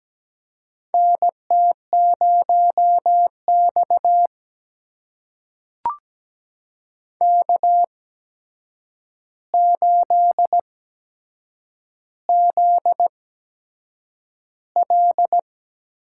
Morse code, heard as a keyed tone, about 17 wpm, spaced as for 7 wpm, 700 Hz. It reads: NT0X K8ZL
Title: USA Call Signs - 4 Characters - 8x Character Spacing - 17wpm